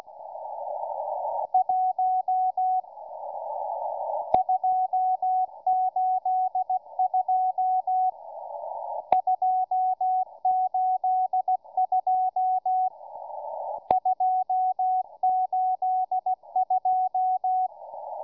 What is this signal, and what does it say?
Morse code, heard as a keyed tone, about 15 wpm, 740 Hz, with radio noise. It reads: 1 282 282 282